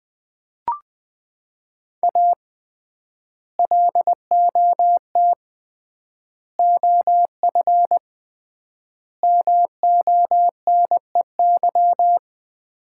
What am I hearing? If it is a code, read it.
Morse code, heard as a keyed tone, about 21 wpm, 700 Hz. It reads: A LOT OF MONEY
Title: 3x Word-Spacing - Sets of 4 Words - 20wpm